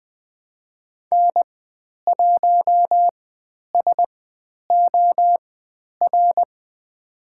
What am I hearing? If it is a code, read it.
Morse code, heard as a keyed tone, about 20 wpm, spaced as for 10 wpm, 700 Hz. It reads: N1SOR
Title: 109 Lesson 18g - Letter m - Call Signs 20wpm